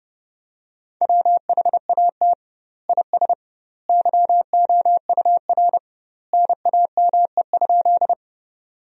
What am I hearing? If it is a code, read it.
Morse code, heard as a keyed tone, about 32 wpm, 700 Hz. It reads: WHAT IS YOUR NAME?
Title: 2x Word-Spacing - Sentences from Top 200 Words - 30wpm